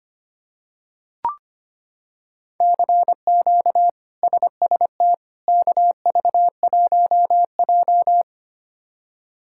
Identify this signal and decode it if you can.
Morse code, heard as a keyed tone, about 25 wpm, 700 Hz. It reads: CQ SST KV1J
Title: K1USN SST QSO Practice - 25wpm